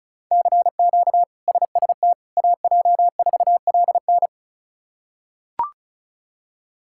Morse code, heard as a keyed tone, about 34 wpm, 700 Hz. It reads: CQ SST AJ4LN
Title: K1USN SST QSO Practice - 35wpm